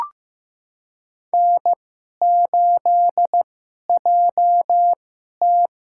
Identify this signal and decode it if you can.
Morse code, heard as a keyed tone, about 15 wpm, spaced as for 11 wpm, 700 Hz. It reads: N8JT
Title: USA Call Signs - 4 Characters - 2x Character Spacing - 15wpm